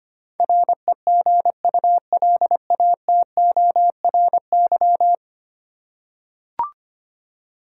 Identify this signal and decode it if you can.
Morse code, heard as a keyed tone, about 25 wpm, 700 Hz. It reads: REGULATORY
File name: practice / 25wpm / Ten Letter Words - 25wpm